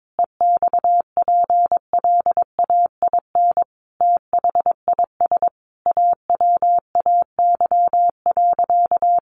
Morse code, heard as a keyed tone, about 22 wpm, 700 Hz. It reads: EXPLAIN THIS AWAY.
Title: Sentences from Top 1100 Words 22wpm